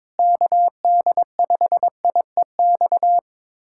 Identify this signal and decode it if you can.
Morse code, heard as a keyed tone, about 22 wpm, 700 Hz. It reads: KD5IEX
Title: USA Callsign DE Callsign 22wpm